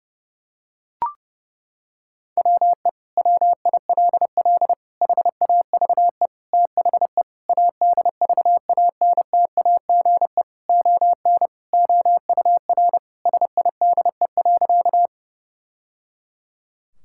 Morse code, heard as a keyed tone, about 30 wpm, 700 Hz. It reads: WE WILL HAVE THE ADVANTAGE ON OUR SIDE.
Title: Sentences from Top 1000 Words - Review Part 6 - 30wpm